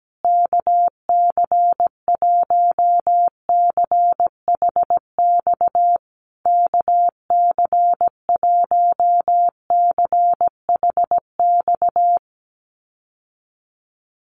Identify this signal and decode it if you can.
Morse code, heard as a keyed tone, about 17 wpm, 700 Hz. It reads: KC1CHX KC1CHX
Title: USA Call Signs Repeated 3x 17wpm